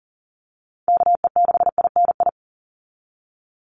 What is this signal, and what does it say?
Morse code, heard as a keyed tone, about 40 wpm, 700 Hz. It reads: KE6INI
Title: USA Call Signs 40wpm